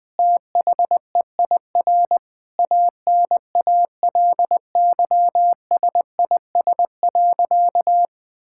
Morse code, heard as a keyed tone, about 20 wpm, 700 Hz. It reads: THEIR ANALYSIS.
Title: Sentences from Top 1000 Words - Review Part 5 - 20wpm